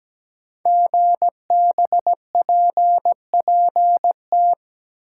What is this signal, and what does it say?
Morse code, heard as a keyed tone, about 17 wpm, 700 Hz. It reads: GBPPT
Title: Code Groups - 5 Letters - 17wpm